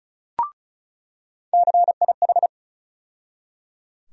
Morse code, heard as a keyed tone, about 36 wpm, 700 Hz. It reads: CIH
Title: USA Call Sign Postfixes - 3 Letters 35wpm